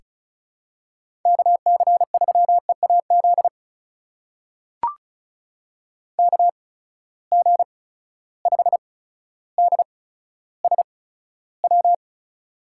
Morse code, heard as a keyed tone, about 34 wpm, spaced as for 15 wpm, 700 Hz. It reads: KC3EAZ KG5DSW
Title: USA Call Signs - 6 Characters - 8x Character Spacing - 35wpm